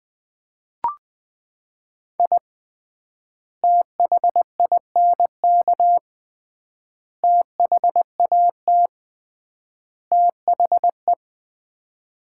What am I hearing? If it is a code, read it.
Morse code, heard as a keyed tone, about 20 wpm, 700 Hz. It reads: I THINK THAT THE
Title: 3x Word-Spacing - Sets of 4 Words - 20wpm